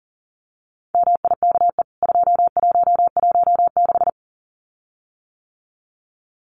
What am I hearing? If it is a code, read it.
Morse code, heard as a keyed tone, about 39 wpm, 700 Hz. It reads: MIKE 2116
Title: CWT Exchange 40wpm